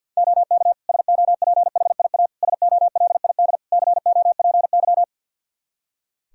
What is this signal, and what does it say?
Morse code, heard as a keyed tone, about 50 wpm, 700 Hz. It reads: OK SOPHIA SOLID COPY
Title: Practice QSO - Exchange 3 - More Info - 50wpm